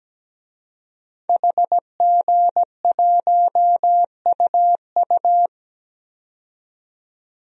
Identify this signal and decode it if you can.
Morse code, heard as a keyed tone, about 17 wpm, 700 Hz. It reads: HG1UU
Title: DXCC-100 Call Signs - 5 Characters - 17wpm